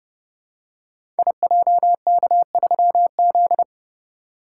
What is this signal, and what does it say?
Morse code, heard as a keyed tone, about 28 wpm, 700 Hz. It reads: IJK3Z
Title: Code Groups - 5 Letter:Numbers - 30wpm